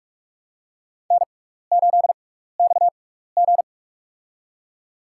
Morse code, heard as a keyed tone, about 44 wpm, spaced as for 15 wpm, 700 Hz. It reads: N8XG
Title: USA Call Signs - 6x Character Spacing - 45wpm